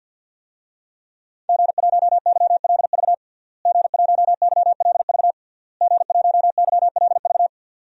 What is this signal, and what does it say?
Morse code, heard as a keyed tone, about 52 wpm, 700 Hz. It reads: G1YLV G1YLV G1YLV
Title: DXCC-100 Call Signs - 5 Characters - Repeated 3x - 50wpm